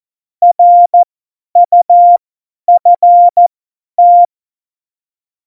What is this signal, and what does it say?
Morse code, heard as a keyed tone, about 14 wpm, spaced as for 10 wpm, 700 Hz. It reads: RUFT